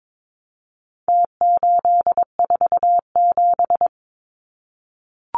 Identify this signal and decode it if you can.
Morse code, heard as a keyed tone, about 22 wpm, 700 Hz. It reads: T847
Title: Numbers - Contest Style - 4-Digit 22wpm